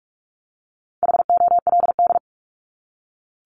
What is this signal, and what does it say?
Morse code, heard as a keyed tone, about 44 wpm, 700 Hz. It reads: HOLD
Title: ING Suffix - Top 100 Words with Base Words 45wpm